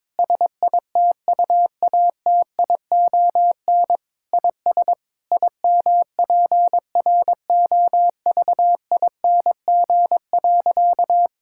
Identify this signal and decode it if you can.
Morse code, heard as a keyed tone, about 22 wpm, 700 Hz. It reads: SITUATION IS IMPROVING.
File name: practice / 22wpm / Sentences with 8-Word Phrases - 22wpm